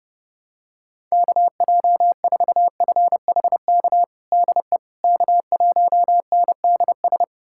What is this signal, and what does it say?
Morse code, heard as a keyed tone, about 30 wpm, 690 Hz. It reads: KJ4FHK DE K1NDS
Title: USA Callsign DE Callsign 30wpm